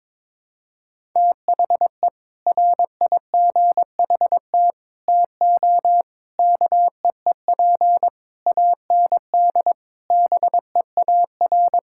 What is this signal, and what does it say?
Morse code, heard as a keyed tone, about 22 wpm, 700 Hz. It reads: THE RIGHT TO KEEP AND BEAR
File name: Sentences with 7-Word Phrases - 22wpm